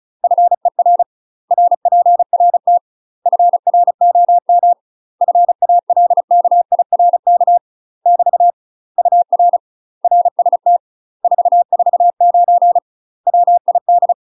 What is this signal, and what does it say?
Morse code, heard as a keyed tone, about 34 wpm, 700 Hz. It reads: FER RPRT FROM FALKIRK <BT> UR RST 449 WID
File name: UK MASTER QSOs TRAINING / 35WPM / UK MASTER QSOs 0005